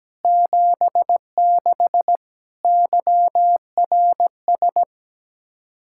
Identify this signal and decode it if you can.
Morse code, heard as a keyed tone, about 17 wpm, 700 Hz. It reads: 76 YRS
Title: Practice QSO - Exchange 3 - More Info - 17wpm